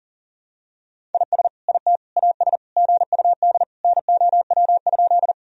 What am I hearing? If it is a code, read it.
Morse code, heard as a keyed tone, about 40 wpm, 700 Hz. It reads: IS IT AS GUD NOW?
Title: Sentences from Top 200 Words and Abbreviations 40wpm